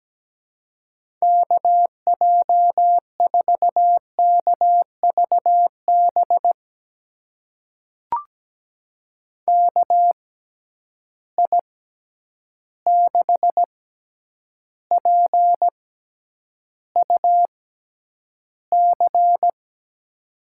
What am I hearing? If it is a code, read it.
Morse code, heard as a keyed tone, about 17 wpm, spaced as for 9 wpm, 700 Hz. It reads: KJ4KVB KI6PUC
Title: USA Call Signs - 6 Characters - 6x Character Spacing - 17wpm